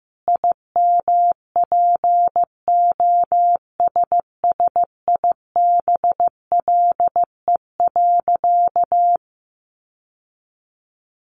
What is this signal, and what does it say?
Morse code, heard as a keyed tone, about 14 wpm, 700 Hz. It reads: IMPOSSIBLE.
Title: Sentences from Top 1100 Words 15wpm